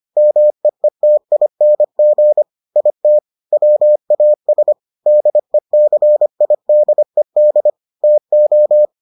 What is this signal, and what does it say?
Morse code, heard as a keyed tone, about 25 wpm, 600 Hz. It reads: MEETING IT WAS DECIDED TO